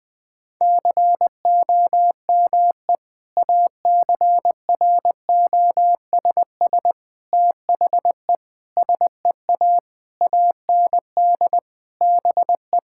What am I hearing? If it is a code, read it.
Morse code, heard as a keyed tone, about 20 wpm, 700 Hz. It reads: COME ACROSS THE SEA AND BE
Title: Sentences from Top 1000 Words - Review Part 5 - 20wpm